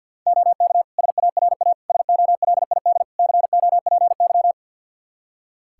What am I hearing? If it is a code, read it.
Morse code, heard as a keyed tone, about 52 wpm, 700 Hz. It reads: OK SARA SOLID COPY
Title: Practice QSO - Exchange 3 - More Info - 50wpm